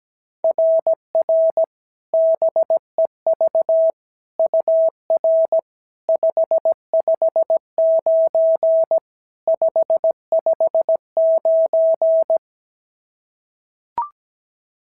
Morse code, heard as a keyed tone, about 15 wpm, 650 Hz. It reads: RR BEV UR 559 559
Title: SOTA QSO Practice 17wpm